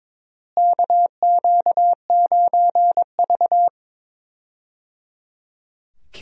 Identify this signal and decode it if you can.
Morse code, heard as a keyed tone, about 22 wpm, 700 Hz. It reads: KQ9V
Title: USA Call Signs - 4 Characters - Repeated 3x - 22wpm